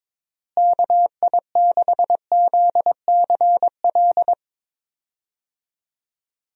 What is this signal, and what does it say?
Morse code, heard as a keyed tone, about 21 wpm, 700 Hz. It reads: KI6ZCL
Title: USA Call Signs - 6 Characters - Repeated 2x - 22wpm